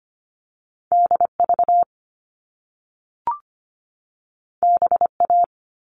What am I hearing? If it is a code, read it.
Morse code, heard as a keyed tone, about 25 wpm, 700 Hz. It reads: DV BA
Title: All Cut Numbers - 2-Digit 25wpm